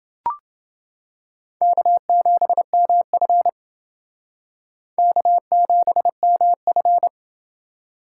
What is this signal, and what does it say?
Morse code, heard as a keyed tone, about 28 wpm, 700 Hz. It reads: K7MF K7MF